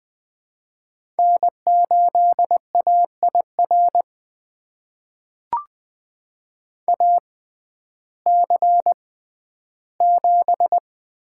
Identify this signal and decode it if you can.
Morse code, heard as a keyed tone, about 21 wpm, spaced as for 12 wpm, 700 Hz. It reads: N8AIR AC7
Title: USA Call Signs - 5 Characters - 6x Character Spacing - 20wpm